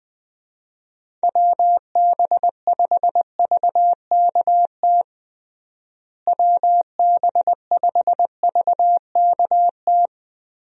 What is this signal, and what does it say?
Morse code, heard as a keyed tone, about 21 wpm, 700 Hz. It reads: WB5VKT WB5VKT
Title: USA Call Signs - 6 Characters - Repeated 3x - 20wpm